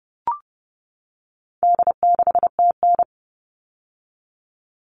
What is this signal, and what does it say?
Morse code, heard as a keyed tone, about 30 wpm, 700 Hz. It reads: D6TN
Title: All Cut Numbers - 4-Digit 30wpm